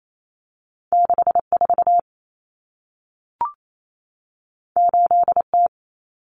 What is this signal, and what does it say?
Morse code, heard as a keyed tone, about 28 wpm, 700 Hz. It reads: 64 8T
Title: Common Cut Numbers - 2-Digit 28wpm